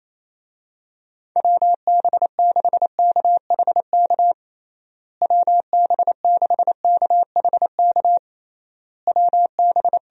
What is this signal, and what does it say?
Morse code, heard as a keyed tone, about 28 wpm, 700 Hz. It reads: WB6KHK WB6KHK WB
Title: USA Call Signs - 6 Characters - Repeated 3x - 28wpm